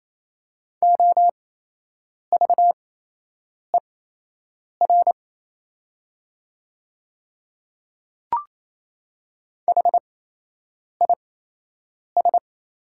Morse code, heard as a keyed tone, about 27 wpm, spaced as for 8 wpm, 690 Hz. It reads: OVER HIS
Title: 8x Character Spacing - Top 200 Words - 28wpm